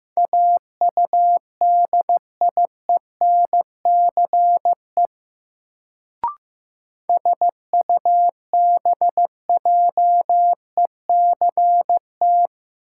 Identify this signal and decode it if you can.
Morse code, heard as a keyed tone, about 15 wpm, 700 Hz. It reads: AUDIENCE SUBJECT
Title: Sentences from Top 800 Words - New Word Drill - 15wpm